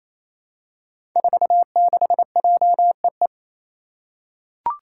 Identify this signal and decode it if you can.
Morse code, heard as a keyed tone, about 28 wpm, 700 Hz. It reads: 46JEE
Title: Five Challenging Letter-Number Combinations 28wpm